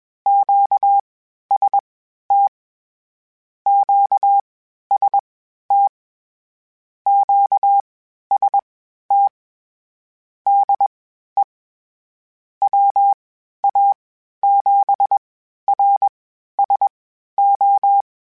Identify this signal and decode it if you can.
Morse code, heard as a keyed tone, about 21 wpm, spaced as for 12 wpm, 800 Hz. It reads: QST QST QST DE WA7RSO